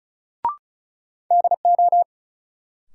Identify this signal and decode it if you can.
Morse code, heard as a keyed tone, about 36 wpm, 700 Hz. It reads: DO
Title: Sentences from Top 100 Words - New Word Drill - 35wpm